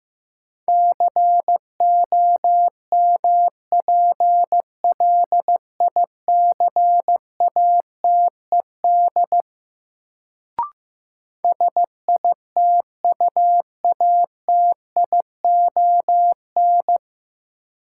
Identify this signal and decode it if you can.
Morse code, heard as a keyed tone, about 15 wpm, 700 Hz. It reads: COMPLICATED SITUATION
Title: Sentences from Top 1200 Words - New Word Drill Alt - 15wpm